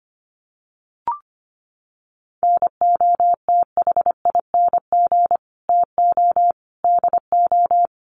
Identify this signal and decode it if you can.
Morse code, heard as a keyed tone, about 25 wpm, 700 Hz. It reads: NOTHING TO DO
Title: Sets of 5 Words 25wpm